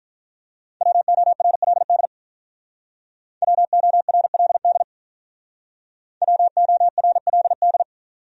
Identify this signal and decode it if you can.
Morse code, heard as a keyed tone, about 48 wpm, 700 Hz. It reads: WORLD WORLD WORLD